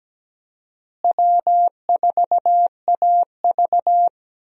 Morse code, heard as a keyed tone, about 17 wpm, 700 Hz. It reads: W4AV
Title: USA Call Signs - 4 Characters - Repeated 3x - 17wpm